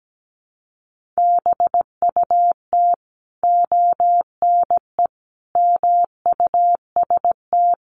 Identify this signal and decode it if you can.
Morse code, heard as a keyed tone, about 17 wpm, 700 Hz. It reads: BUT ONE MUST
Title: Sentences from Top 1400 Words 17wpm